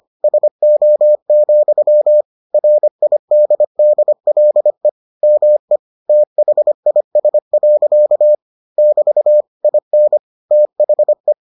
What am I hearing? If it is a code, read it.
Morse code, heard as a keyed tone, about 25 wpm, 600 Hz. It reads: SO, RIDDLE ME THIS. <BT> IN THE